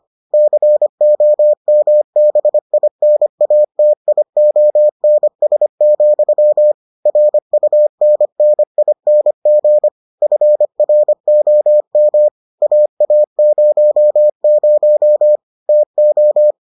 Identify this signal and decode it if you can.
Morse code, heard as a keyed tone, about 25 wpm, 600 Hz. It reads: COMBINATIONS, RUNNING FROM AA00 TO